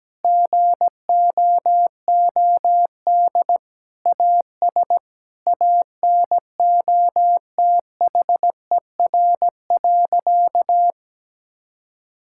Morse code, heard as a keyed tone, about 17 wpm, 700 Hz. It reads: GOOD AS ANOTHER.
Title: Sentences from Top 1600 Words 17wpm